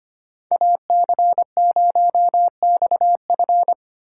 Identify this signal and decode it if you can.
Morse code, heard as a keyed tone, about 30 wpm, 700 Hz. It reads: AC0XF
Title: Speed Racing - USA 2x2 Call Signs 25wpm